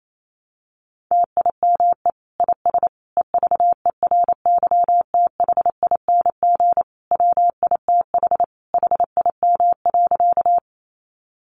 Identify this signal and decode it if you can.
Morse code, heard as a keyed tone, about 28 wpm, 700 Hz. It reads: TIME IS EVERYTHING WITH HIM.
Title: Sentences from Top 700 Words 28wpm